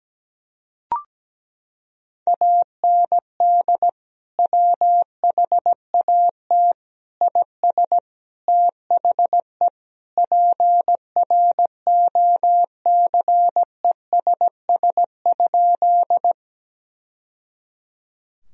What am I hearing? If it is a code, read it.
Morse code, heard as a keyed tone, about 17 wpm, 700 Hz. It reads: AND WHAT IS THE PROCESS?
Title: Sentences from Top 1000 Words - Review Part 3 - 17wpm